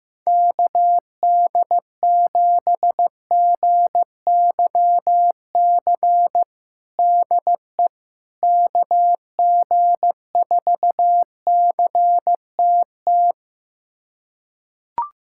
Morse code, heard as a keyed tone, about 15 wpm, 700 Hz. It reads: KD7GYC DE KG4CTT
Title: USA Callsign DE Callsign - 6 Characters - 15wpm